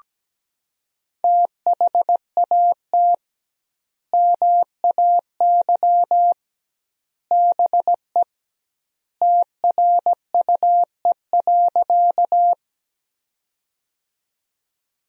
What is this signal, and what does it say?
Morse code, heard as a keyed tone, about 17 wpm, 700 Hz. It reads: THAT MAY BE TRUE.
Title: 2x Word-Spacing - Sentences from Top 400 Words - 17wpm